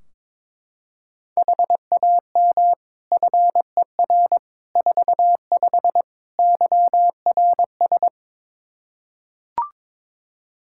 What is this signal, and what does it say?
Morse code, heard as a keyed tone, about 22 wpm, 700 Hz. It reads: HAM FER 45 YRS
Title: Practice QSO - Exchange 3 - More Info - 22wpm